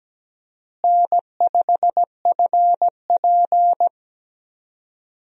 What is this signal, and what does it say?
Morse code, heard as a keyed tone, about 17 wpm, 700 Hz. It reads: N5FP
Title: USA Call Signs - 4 Characters - Repeated 2x - 17wpm